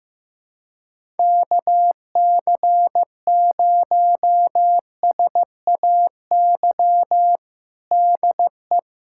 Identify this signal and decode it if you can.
Morse code, heard as a keyed tone, about 15 wpm, 700 Hz. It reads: KC0SAY DE
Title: USA Callsign DE Callsign - 6 Characters - 15wpm